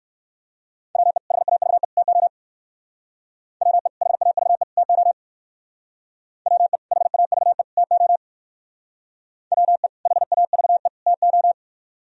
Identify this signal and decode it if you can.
Morse code, heard as a keyed tone, about 58 wpm, 700 Hz. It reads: WE HAVE TO WE HAVE TO WE HAVE TO WE HAVE TO